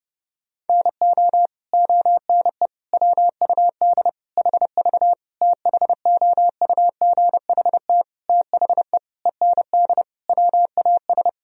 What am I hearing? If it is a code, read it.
Morse code, heard as a keyed tone, about 30 wpm, 700 Hz. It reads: NO ONE WUD HV THOUGHT THE END WAS
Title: Sentences from Top 300 Words and Abbreviations 30wpm